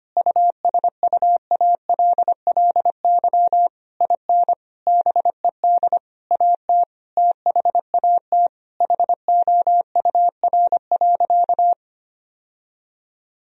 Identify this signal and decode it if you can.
Morse code, heard as a keyed tone, about 25 wpm, 700 Hz. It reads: USUALLY IN BED AT THAT HOUR.
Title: Sentences from Top 1000 Words - Review Part 3 - 25wpm